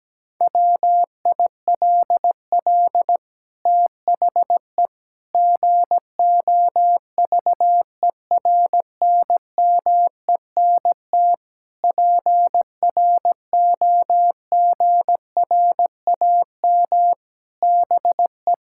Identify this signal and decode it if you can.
Morse code, heard as a keyed tone, about 17 wpm, 700 Hz. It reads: WILL THE GOVERNMENT PROGRAM BE
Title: Sentences from Top 1000 Words - Review Part 5 - 17wpm